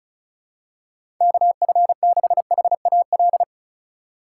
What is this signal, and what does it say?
Morse code, heard as a keyed tone, about 36 wpm, 700 Hz. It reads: KF6HAL
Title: USA Call Signs - 4x Character Spacing - 35wpm